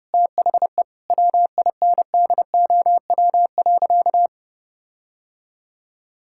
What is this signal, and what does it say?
Morse code, heard as a keyed tone, about 30 wpm, 700 Hz. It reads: THE WINDOW.
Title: Sentences from Top 1600 Words 30wpm